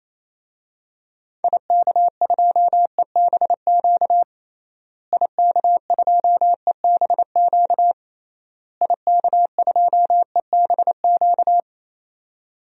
Morse code, heard as a keyed tone, about 28 wpm, 700 Hz. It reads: IK2EBQ IK2EBQ IK2EBQ
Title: DXCC-100 Call Signs - 6 Characters - Repeated 3x - 28wpm